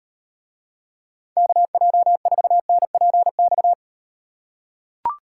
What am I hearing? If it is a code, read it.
Morse code, heard as a keyed tone, about 38 wpm, 700 Hz. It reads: KJ4NPX